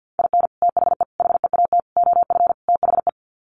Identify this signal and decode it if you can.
Morse code, heard as a keyed tone, about 50 wpm, 700 Hz. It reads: IN THE HEAT OF THE
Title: Sets of 5 Words Encore2 50wpm